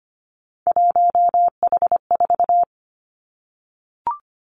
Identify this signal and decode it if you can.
Morse code, heard as a keyed tone, about 25 wpm, 700 Hz. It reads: 1H4
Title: Three Challenging Letter-Number Combinations 25wpm